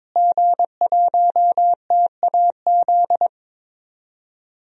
Morse code, heard as a keyed tone, about 21 wpm, 700 Hz. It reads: G1TAZ
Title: DXCC-100 Call Signs - 5 Characters - Repeated 2x - 22wpm